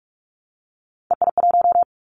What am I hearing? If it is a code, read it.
Morse code, heard as a keyed tone, about 46 wpm, 700 Hz. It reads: EI1
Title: Callsign Prefixes for Top 50 DXCC - 45wpm